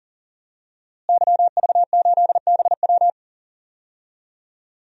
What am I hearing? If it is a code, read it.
Morse code, heard as a keyed tone, about 40 wpm, 700 Hz. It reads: YV8BW
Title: DXCC-100 Call Signs - 5 Characters - Repeated 2x - 40wpm